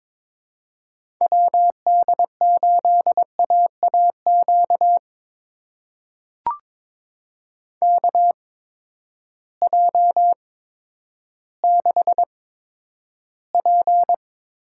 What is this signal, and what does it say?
Morse code, heard as a keyed tone, about 22 wpm, spaced as for 11 wpm, 700 Hz. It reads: WD8AAQ KJ6P
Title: USA Call Signs - 6 Characters - 8x Character Spacing - 22wpm